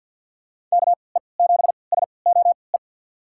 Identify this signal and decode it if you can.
Morse code, heard as a keyed tone, about 52 wpm, spaced as for 29 wpm, 700 Hz. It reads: KE7SOE
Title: USA Call Signs - 6 Characters - 3x Character Spacing - 50wpm